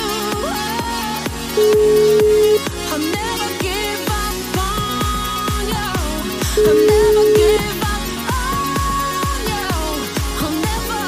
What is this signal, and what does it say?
Ringback tone (Europe, 425 Hz), with noise.